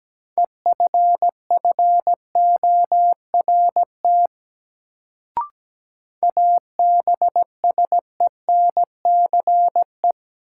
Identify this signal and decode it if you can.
Morse code, heard as a keyed tone, about 17 wpm, 700 Hz. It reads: EFFORT ABSENCE
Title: Sets of 4 Words - New Word Drill - 17wpm